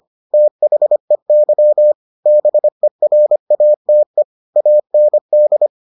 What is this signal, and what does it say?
Morse code, heard as a keyed tone, about 25 wpm, 600 Hz. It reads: THEY BERATE AND